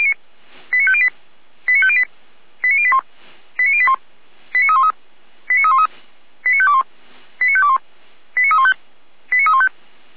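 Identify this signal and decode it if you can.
Five-tone selective call (ZVEI), heard as quick runs of five tones, tones of about 70 ms. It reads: 89508 89508 89081 89081 89313 89313 89521 89521 89315 89315
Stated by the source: ZVEI five-tone call received by SDR.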